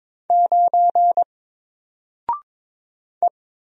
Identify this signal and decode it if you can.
Morse code, heard as a keyed tone, about 22 wpm, 700 Hz. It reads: 9 E
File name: Single-Letter-Number-22wpm